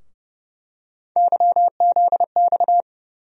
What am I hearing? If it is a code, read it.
Morse code, heard as a keyed tone, about 30 wpm, 700 Hz. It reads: YZX
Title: USA Call Sign Postfixes - 3 Letters 30wpm